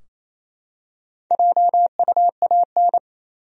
Morse code, heard as a keyed tone, about 28 wpm, 700 Hz. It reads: JUAN
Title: USA Common Names - Max 4 Letters - 28wpm